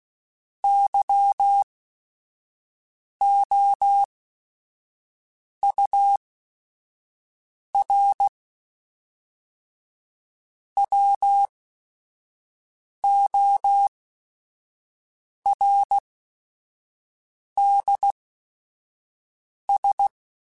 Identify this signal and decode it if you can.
Morse code, heard as a keyed tone, about 16 wpm, spaced as for 5 wpm, 800 Hz. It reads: YOUR WORDS
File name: Chuckles / Chuckles-mp3s / SoftWords @5